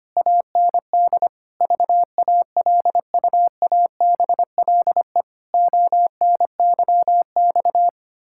Morse code, heard as a keyed tone, about 25 wpm, 700 Hz. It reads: AND VALUABLE ONYX
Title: Pangram Challenge v2 - 25wpm